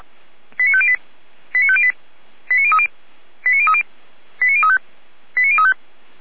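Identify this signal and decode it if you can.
Five-tone selective call (ZVEI), heard as quick runs of five tones, tones of about 70 ms. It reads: 89508 89508 89030 89030 89035 89035
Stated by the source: ZVEI five-tone selective call, SDR reception.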